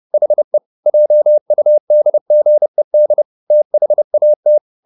Morse code, heard as a keyed tone, about 30 wpm, 600 Hz. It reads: HE JUDGED THAT